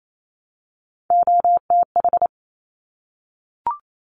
Morse code, heard as a keyed tone, about 28 wpm, 700 Hz. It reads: OTH
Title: Top Three Letter Combinations in Words 28wpm